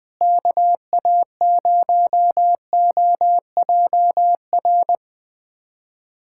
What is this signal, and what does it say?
Morse code, heard as a keyed tone, about 20 wpm, 690 Hz. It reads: KA0OJR
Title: USA Callsign DE Callsign 20wpm